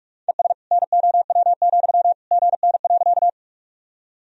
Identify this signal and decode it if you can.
Morse code, heard as a keyed tone, about 42 wpm, 700 Hz. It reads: ES NOW, GN.
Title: Sentences from Top 300 Words and Abbreviations 45wpm